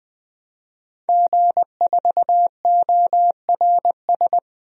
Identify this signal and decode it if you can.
Morse code, heard as a keyed tone, about 20 wpm, 700 Hz. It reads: G4ORS
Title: DXCC-100 Call Signs - 5 Characters - 20wpm